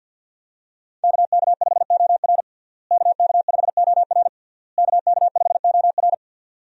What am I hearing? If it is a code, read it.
Morse code, heard as a keyed tone, about 50 wpm, 700 Hz. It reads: KK5OR KK5OR KK5OR
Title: USA Call Signs - 5 Characters - Repeated 3x - 50wpm